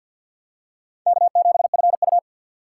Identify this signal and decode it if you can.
Morse code, heard as a keyed tone, about 51 wpm, 700 Hz. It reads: K7FU
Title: USA Call Signs - 4 Characters - 3x Character Spacing - 50wpm